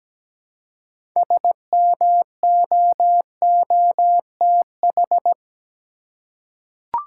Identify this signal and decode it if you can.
Morse code, heard as a keyed tone, about 17 wpm, 700 Hz. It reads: SMOOTH
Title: Six Letter Words - 17wpm